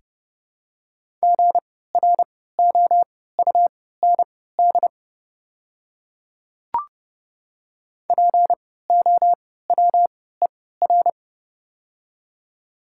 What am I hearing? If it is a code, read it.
Morse code, heard as a keyed tone, about 30 wpm, spaced as for 17 wpm, 700 Hz. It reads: GROUND POWER
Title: 3x Character Spacing - Top 500 Words - 30wpm